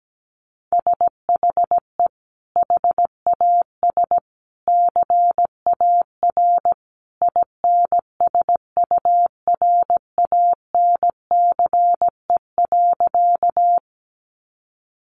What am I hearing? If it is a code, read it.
Morse code, heard as a keyed tone, about 17 wpm, 700 Hz. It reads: SHE HAS CAR INSURANCE.
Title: Sentences from Top 1100 Words 17wpm